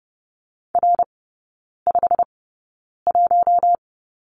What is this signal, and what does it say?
Morse code, heard as a keyed tone, about 30 wpm, spaced as for 9 wpm, 700 Hz. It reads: R51
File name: course / 30:10wpm / 087 Lesson 15e - Number 5 - Three Characters 30wpm